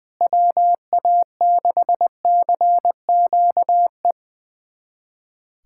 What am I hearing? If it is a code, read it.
Morse code, heard as a keyed tone, about 21 wpm, 700 Hz. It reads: WA6CQE
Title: USA Callsign DE Callsign 20wpm